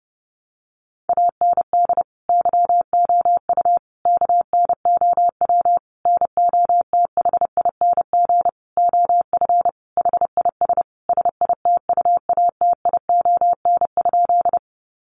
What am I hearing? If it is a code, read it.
Morse code, heard as a keyed tone, about 30 wpm, 700 Hz. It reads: AND YOU KNOW NOTHING OF HIS SITUATION?
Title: Sentences from Top 1200 Words 30wpm